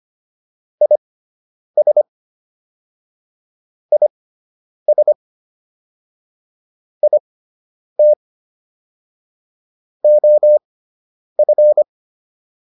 Morse code, heard as a keyed tone, about 25 wpm, spaced as for 9 wpm, 600 Hz. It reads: IS IS IT OF